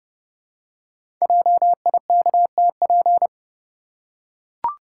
Morse code, heard as a keyed tone, about 28 wpm, 700 Hz. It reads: JIKTP
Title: Code Groups - 5 Letters - 30wpm